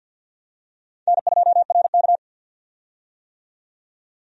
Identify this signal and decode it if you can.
Morse code, heard as a keyed tone, about 47 wpm, 700 Hz. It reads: N2RK
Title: USA Call Signs - 4 Characters - 50wpm